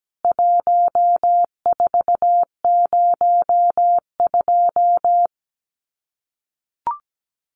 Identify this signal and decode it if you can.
Morse code, heard as a keyed tone, about 17 wpm, 700 Hz. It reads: 1402
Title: CWT Exchange 17wpm